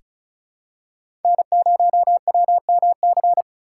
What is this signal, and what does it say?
Morse code, heard as a keyed tone, about 34 wpm, 700 Hz. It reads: N0WMC
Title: USA Call Signs - 5 Characters - 4x Character Spacing - 35wpm